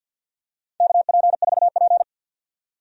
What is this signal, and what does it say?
Morse code, heard as a keyed tone, about 51 wpm, 700 Hz. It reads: KP4P
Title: USA Call Signs - 4 Characters - 50wpm